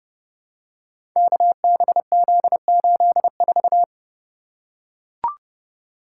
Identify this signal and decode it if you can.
Morse code, heard as a keyed tone, about 30 wpm, 700 Hz. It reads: KBZ84
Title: Code Groups - 5 Letter:Numbers - 30wpm